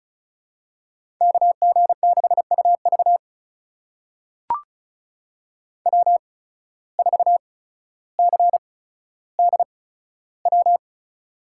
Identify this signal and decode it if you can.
Morse code, heard as a keyed tone, about 36 wpm, spaced as for 15 wpm, 700 Hz. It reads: KG6UV W4CDW
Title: USA Call Signs - 5 Characters - 8x Character Spacing - 35wpm